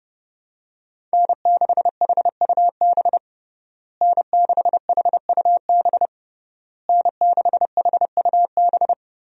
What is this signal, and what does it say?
Morse code, heard as a keyed tone, about 30 wpm, 700 Hz. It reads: N6HUB N6HUB N6HUB
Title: USA Call Signs - 5 Characters - Repeated 3x - 30wpm